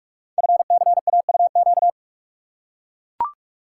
Morse code, heard as a keyed tone, about 44 wpm, 690 Hz. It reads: FCAUQ
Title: Code Groups - 5 Letters - 45wpm